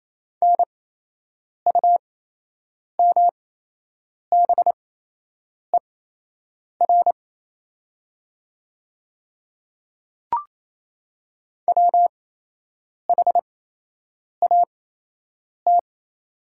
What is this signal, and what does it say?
Morse code, heard as a keyed tone, about 28 wpm, spaced as for 8 wpm, 700 Hz. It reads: NUMBER WHAT
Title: 8x Character Spacing - Top 100 Words - 28wpm